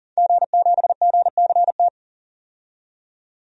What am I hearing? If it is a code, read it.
Morse code, heard as a keyed tone, about 40 wpm, 700 Hz. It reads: G7GCT
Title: DXCC-100 Call Signs - 5 Characters - 40wpm